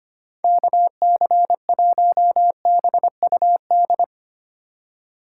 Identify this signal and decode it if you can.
Morse code, heard as a keyed tone, about 25 wpm, 700 Hz. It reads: KC1BUD
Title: USA Call Signs - 6 Characters - 6x Character Spacing - 25wpm